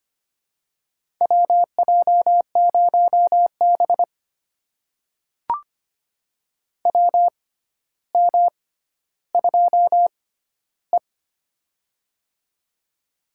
Morse code, heard as a keyed tone, about 25 wpm, spaced as for 13 wpm, 700 Hz. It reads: WJ0B WM2E
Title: USA Call Signs - 4 Characters - 6x Character Spacing - 25wpm